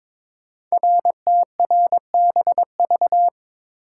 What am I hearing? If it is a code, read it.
Morse code, heard as a keyed tone, about 22 wpm, 700 Hz. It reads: RTRBV
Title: Code Groups - 5 Letters - 22wpm